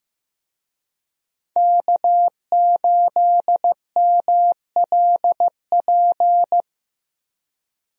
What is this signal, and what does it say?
Morse code, heard as a keyed tone, about 15 wpm, 700 Hz. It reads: K8MLP
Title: USA Call Signs - 5 Characters - 15wpm